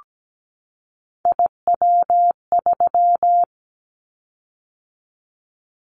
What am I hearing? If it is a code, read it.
Morse code, heard as a keyed tone, about 17 wpm, 700 Hz. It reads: IW3
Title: Callsign Prefixes for Top 50 DXCC - 17wpm